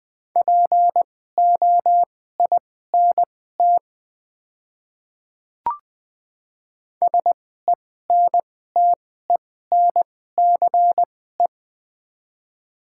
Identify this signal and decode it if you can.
Morse code, heard as a keyed tone, about 20 wpm, spaced as for 15 wpm, 700 Hz. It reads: POINT SENTENCE
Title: 2x Character Spacing - Top 200 Words - 20wpm